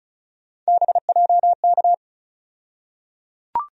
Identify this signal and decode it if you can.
Morse code, heard as a keyed tone, about 35 wpm, 700 Hz. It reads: BJK
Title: USA Call Sign Postfixes - 3 Letters 35wpm